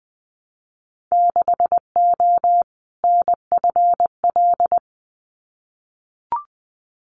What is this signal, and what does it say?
Morse code, heard as a keyed tone, about 20 wpm, 700 Hz. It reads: 6O NFL
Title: Winter Field Day Exchanges 20wpm